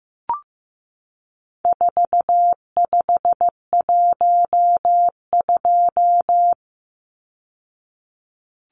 Morse code, heard as a keyed tone, about 15 wpm, 700 Hz. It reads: 4512
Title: Numbers - 4-Digit 15wpm